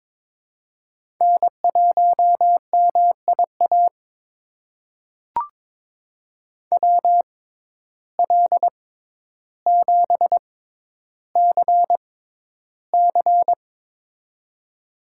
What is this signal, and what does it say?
Morse code, heard as a keyed tone, about 22 wpm, spaced as for 11 wpm, 700 Hz. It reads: N1MIA WL7CC
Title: USA Call Signs - 5 Characters - 6x Character Spacing - 22wpm